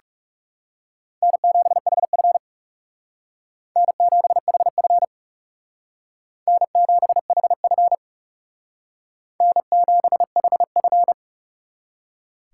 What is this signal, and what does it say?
Morse code, heard as a keyed tone, about 36 wpm, 700 Hz. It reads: N7HF N7HF N7HF N7HF
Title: Speed Racing - USA 1x2 Call Signs 30wpm